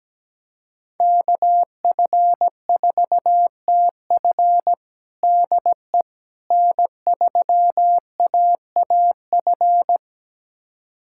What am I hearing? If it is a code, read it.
Morse code, heard as a keyed tone, about 17 wpm, 700 Hz. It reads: KF4TF DE N3AAF
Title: USA Callsign DE Callsign - 5 Characters - 17wpm